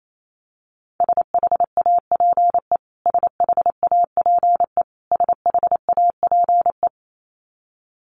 Morse code, heard as a keyed tone, about 28 wpm, 700 Hz. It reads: SHAPE SHAPE SHAPE
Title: Top 500 Words Repeated 28wpm